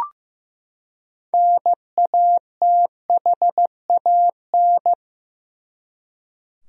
Morse code, heard as a keyed tone, about 15 wpm, 700 Hz. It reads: NATHAN
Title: USA Common Names - 15wpm